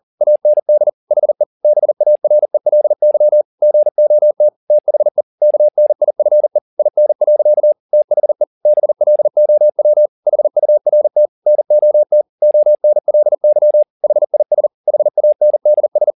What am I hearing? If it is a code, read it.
Morse code, heard as a keyed tone, about 40 wpm, 600 Hz. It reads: AND HE BARELY GOT THE KNIFE IN. THE BLOW HURT NOT ONLY HIS HANDS
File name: the old man and the sea 2 40wpm